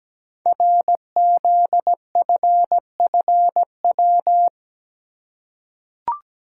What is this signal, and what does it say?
Morse code, heard as a keyed tone, about 17 wpm, 700 Hz. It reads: RZFFW
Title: Code Groups - 5 Letters - 17wpm